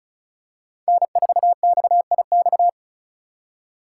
Morse code, heard as a keyed tone, about 36 wpm, 700 Hz. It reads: N4XIX
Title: USA Call Signs - 3x Character Spacing - 35wpm